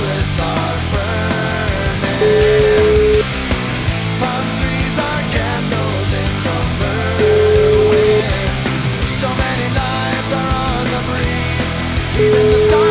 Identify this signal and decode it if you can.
Ringback tone (Europe, 425 Hz), with noise.